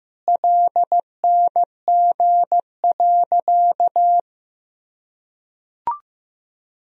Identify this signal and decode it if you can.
Morse code, heard as a keyed tone, about 15 wpm, 700 Hz. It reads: LNG.
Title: Sentences from Top 100 Words and Abbreviations 15wpm